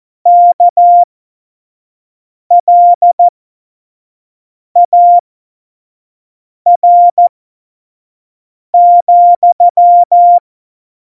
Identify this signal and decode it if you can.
Morse code, heard as a keyed tone, about 14 wpm, spaced as for 5 wpm, 700 Hz. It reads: KLAR,
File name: Morsekurs / Witze / Witz 13. Oktober 2020, 25 F70